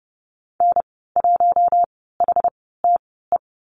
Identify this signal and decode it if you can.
Morse code, heard as a keyed tone, about 30 wpm, spaced as for 17 wpm, 700 Hz. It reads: N1HTE
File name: Lesson 14g - Number 2 - Call Signs 30wpm